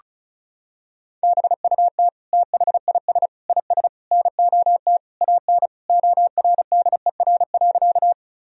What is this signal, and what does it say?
Morse code, heard as a keyed tone, about 34 wpm, 700 Hz. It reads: BUT THIS IS NOT AN ORDER.
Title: Sentences from Top 1000 Words - Review Part 2 - 35wpm